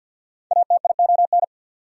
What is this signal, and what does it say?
Morse code, heard as a keyed tone, about 48 wpm, 700 Hz. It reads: ATION
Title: Prefixes and Suffixes 50wpm